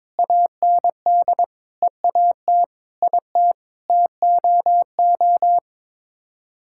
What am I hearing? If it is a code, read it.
Morse code, heard as a keyed tone, about 22 wpm, 700 Hz. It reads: AND EAT IT TOO
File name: practice / 22wpm / Sets of 8 Words - 22wpm